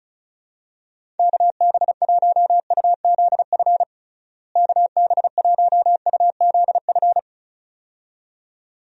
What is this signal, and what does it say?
Morse code, heard as a keyed tone, about 34 wpm, 700 Hz. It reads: KB1UZF KB1UZF
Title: USA Call Signs - 6 Characters - Repeated 2x - 35wpm